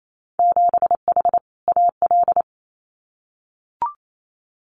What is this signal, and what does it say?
Morse code, heard as a keyed tone, about 28 wpm, 700 Hz. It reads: 7H AL
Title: Winter Field Day Exchanges Encore 28wpm